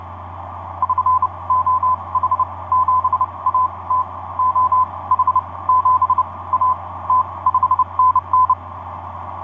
Morse code, heard as a keyed tone, about 29 wpm, spaced as for 21 wpm, 1020 Hz. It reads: <SN> OH7AT OH7AT 5NN
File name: z21bb g4xrx etc 10m cw 23oct2010 1201